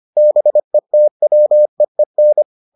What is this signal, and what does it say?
Morse code, heard as a keyed tone, about 25 wpm, 600 Hz. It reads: BETWEEN